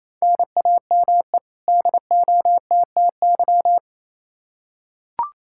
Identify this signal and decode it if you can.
Morse code, heard as a keyed tone, about 28 wpm, 700 Hz. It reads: NAME DOTTY
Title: K1USN SST QSO Practice 28wpm